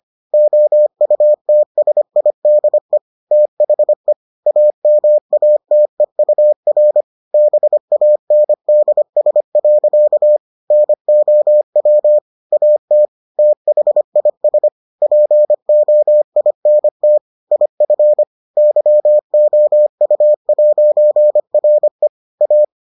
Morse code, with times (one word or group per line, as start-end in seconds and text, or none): 0.3-3.0 OUTSIDE
3.3-4.1 THE
4.5-7.0 AMATEUR
7.3-10.4 BANDS.
10.7-12.2 NOW
12.5-13.1 AT
13.4-14.7 THIS
15.0-17.2 POINT
17.5-18.2 IF
18.6-22.1 YOU'RE
22.4-22.6 A